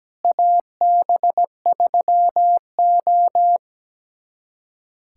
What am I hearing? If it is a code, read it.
Morse code, heard as a keyed tone, about 17 wpm, 700 Hz. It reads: AB3O
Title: USA Callsign DE Callsign - 4 Characters - 17wpm